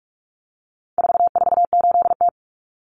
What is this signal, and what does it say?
Morse code, heard as a keyed tone, about 44 wpm, 700 Hz. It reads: 448T
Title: Common Cut Numbers - 4-Digit 45wpm